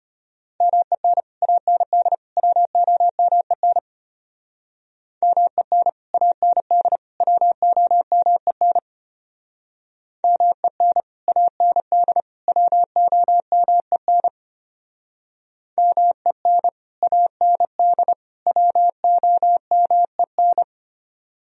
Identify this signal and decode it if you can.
Morse code, heard as a keyed tone, about 32 wpm, 700 Hz. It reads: MEN AND WOMEN MEN AND WOMEN MEN AND WOMEN MEN AND WOMEN